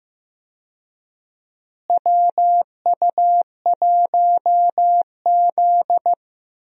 Morse code, heard as a keyed tone, about 15 wpm, 700 Hz. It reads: WU1Z
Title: USA Call Signs - 4 Characters - Repeated 3x - 15wpm